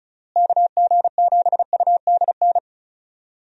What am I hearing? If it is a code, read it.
Morse code, heard as a keyed tone, about 34 wpm, 700 Hz. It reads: KG7UDN
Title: USA Call Signs - 6 Characters - Repeated 3x - 35wpm